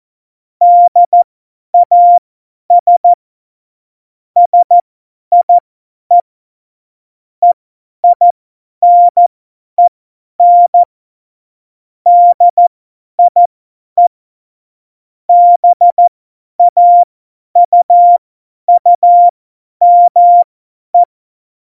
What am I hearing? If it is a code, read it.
Morse code, heard as a keyed tone, about 14 wpm, spaced as for 10 wpm, 700 Hz. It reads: DAS SIE EINEN DIE BAUUME
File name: Morsekurs / Witze / Witz 11. September 2018, 50 F70